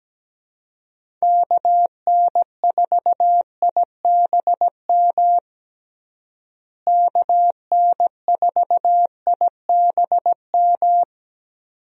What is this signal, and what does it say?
Morse code, heard as a keyed tone, about 17 wpm, 700 Hz. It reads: KN4IBM KN4IBM
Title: USA Call Signs - 6 Characters - Repeated 2x - 17wpm